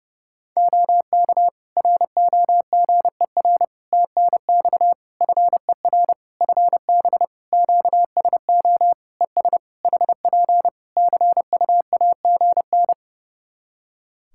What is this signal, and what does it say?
Morse code, heard as a keyed tone, about 30 wpm, 700 Hz. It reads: OK ROGER TNX FER FB QSO ES HP CUAGN
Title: Practice QSO - Exchange 4 - Ending - 30wpm